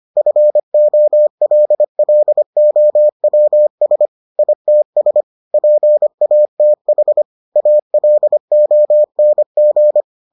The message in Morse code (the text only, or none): FOLLOWS ITS PATH ALONG